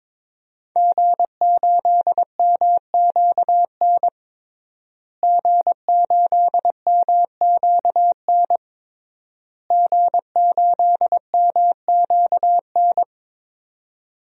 Morse code, heard as a keyed tone, about 22 wpm, 700 Hz. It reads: G8MQN G8MQN G8MQN
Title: DXCC-100 Call Signs - 5 Characters - Repeated 3x - 22wpm